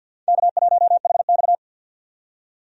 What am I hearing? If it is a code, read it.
Morse code, heard as a keyed tone, about 52 wpm, 700 Hz. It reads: K1HX
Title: USA Call Signs - 4 Characters - 50wpm